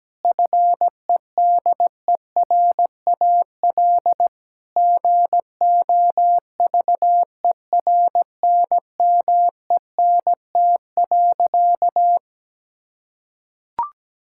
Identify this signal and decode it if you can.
Morse code, heard as a keyed tone, about 17 wpm, 700 Hz. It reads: FEDERAL GOVERNMENT.
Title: Sentences from Top 1000 Words - Review Part 5 - 17wpm